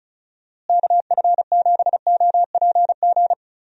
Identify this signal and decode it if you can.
Morse code, heard as a keyed tone, about 36 wpm, 700 Hz. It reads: KF7OPG
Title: USA Call Signs - 6 Characters - Repeated 3x - 35wpm